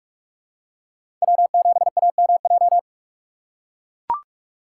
Morse code, heard as a keyed tone, about 44 wpm, 700 Hz. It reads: W7AMJ
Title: USA Call Signs - 5 Characters - 2x Character Spacing - 45wpm